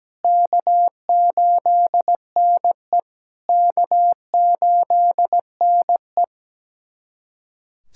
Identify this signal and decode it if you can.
Morse code, heard as a keyed tone, about 17 wpm, 700 Hz. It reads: K8NE K8NE
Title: SOTA QSO Practice 17wpm